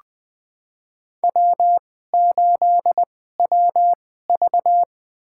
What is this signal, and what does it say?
Morse code, heard as a keyed tone, about 20 wpm, spaced as for 14 wpm, 700 Hz. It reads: W8WV
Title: USA Call Signs - 4 Characters - 2x Character Spacing - 20wpm